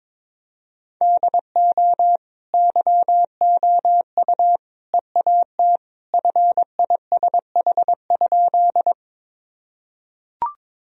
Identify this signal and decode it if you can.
Morse code, heard as a keyed tone, about 22 wpm, 700 Hz. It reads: DO YOU EAT FISH?
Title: Sentences from Top 1000 Words - Review Part 1 - 22wpm